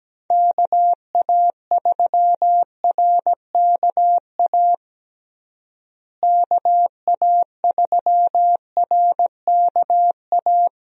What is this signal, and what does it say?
Morse code, heard as a keyed tone, about 17 wpm, 700 Hz. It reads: KA3RKA KA3RKA
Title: USA Call Signs - 6 Characters - Repeated 2x - 17wpm